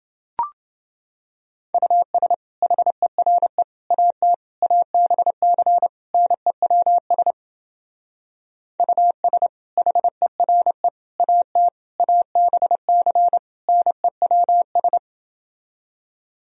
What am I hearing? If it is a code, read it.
Morse code, heard as a keyed tone, about 28 wpm, 700 Hz. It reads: US HERE AT ABC NEWS US HERE AT ABC NEWS